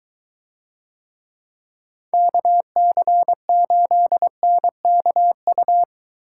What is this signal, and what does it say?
Morse code, heard as a keyed tone, about 21 wpm, 700 Hz. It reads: KC8NKU